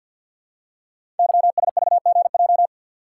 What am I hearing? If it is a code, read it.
Morse code, heard as a keyed tone, about 47 wpm, 700 Hz. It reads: YSVGJ
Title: Code Groups - 5 Letters - 50wpm